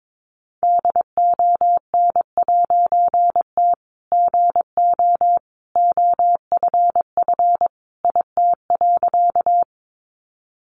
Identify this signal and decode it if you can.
Morse code, heard as a keyed tone, about 22 wpm, 700 Hz. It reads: DON'T GO OFF IT.
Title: A Sound Of Thunder 22wpm